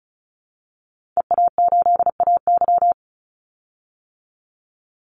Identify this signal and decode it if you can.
Morse code, heard as a keyed tone, about 34 wpm, 700 Hz. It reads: EA8AY
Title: Call Signs - 35wpm